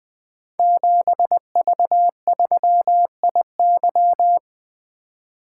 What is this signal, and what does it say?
Morse code, heard as a keyed tone, about 20 wpm, 700 Hz. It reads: 7V3IY
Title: Code Groups - 5 Letter:Numbers - 20wpm